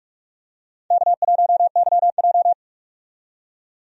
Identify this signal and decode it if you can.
Morse code, heard as a keyed tone, about 43 wpm, 700 Hz. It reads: K1YJ
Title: USA Call Signs - 4 Characters - 45wpm